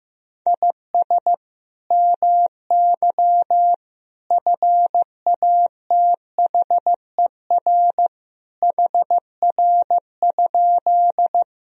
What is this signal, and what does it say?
Morse code, heard as a keyed tone, about 15 wpm, 700 Hz. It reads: IS MY FATHER HR?
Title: Sentences from Top 300 Words and Abbreviations 15wpm